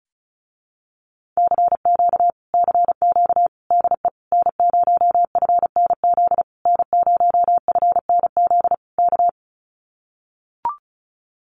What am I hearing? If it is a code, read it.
Morse code, heard as a keyed tone, about 36 wpm, 700 Hz. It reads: CQ CQ DE N0FNZ N0FNZ K
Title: Common QSO Phrases Encore 35wpm